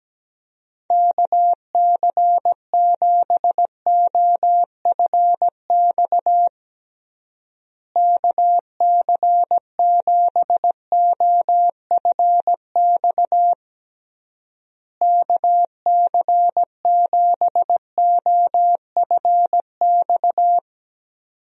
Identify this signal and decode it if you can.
Morse code, heard as a keyed tone, about 17 wpm, 700 Hz. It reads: KC7OFX KC7OFX KC7OFX
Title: USA Call Signs - 6 Characters - Repeated 3x - 17wpm